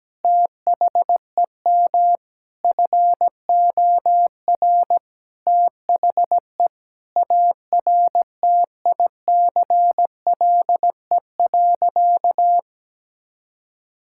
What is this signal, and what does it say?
Morse code, heard as a keyed tone, about 17 wpm, 700 Hz. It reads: THEM FOR THE ARTICLE.
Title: Sentences from Top 1000 Words - Review Part 5 - 17wpm